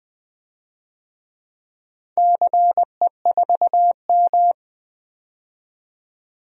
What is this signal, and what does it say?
Morse code, heard as a keyed tone, about 20 wpm, 700 Hz. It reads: CE4M
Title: DXCC-100 Call Signs - 4 Characters - 20wpm